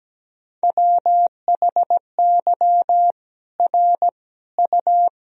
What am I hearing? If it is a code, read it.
Morse code, heard as a keyed tone, about 17 wpm, 700 Hz. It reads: WHY R U
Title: Sentences from Top 200 Words and Abbreviations 17wpm